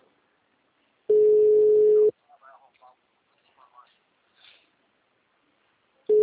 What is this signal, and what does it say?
Ringback tone (Europe, 425 Hz), with speech.